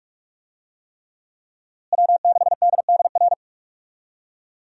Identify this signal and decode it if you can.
Morse code, heard as a keyed tone, about 45 wpm, 700 Hz. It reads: W6DDR
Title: USA Call Signs - 5 Characters - Repeated 3x - 45wpm